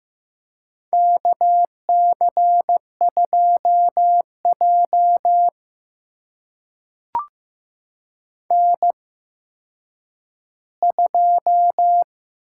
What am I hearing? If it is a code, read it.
Morse code, heard as a keyed tone, about 15 wpm, spaced as for 9 wpm, 700 Hz. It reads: KC2J N2
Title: USA Call Signs - 8x Character Spacing - 15wpm